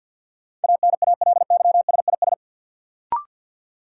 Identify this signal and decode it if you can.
Morse code, heard as a keyed tone, about 53 wpm, 700 Hz. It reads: ANALYSIS
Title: Sentences from Top 1400 Words - New Word Drill Alt - 50wpm